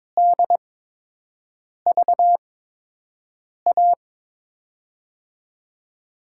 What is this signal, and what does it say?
Morse code, heard as a keyed tone, about 21 wpm, spaced as for 6 wpm, 700 Hz. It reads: DVA
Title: USA Call Signs - 6 Characters - 8x Character Spacing - 22wpm